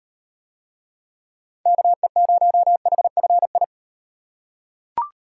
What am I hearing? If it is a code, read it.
Morse code, heard as a keyed tone, about 38 wpm, 700 Hz. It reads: KE0HFI